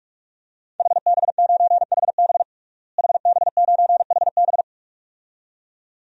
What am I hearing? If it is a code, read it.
Morse code, heard as a keyed tone, about 46 wpm, 700 Hz. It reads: HB9HB HB9HB
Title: DXCC-100 Call Signs - 5 Characters - Repeated 2x - 45wpm